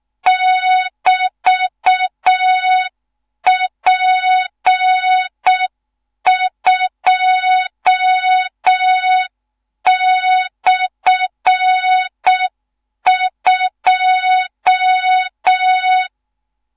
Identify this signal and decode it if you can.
Morse code, heard as a keyed tone, about 6 wpm, 3700 Hz. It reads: <BT>P2/2
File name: Morze-HA2MN / hang / 14 280 HA2MN